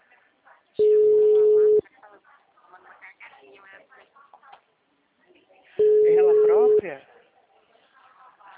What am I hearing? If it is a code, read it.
Ringback tone (Europe, 425 Hz), with speech.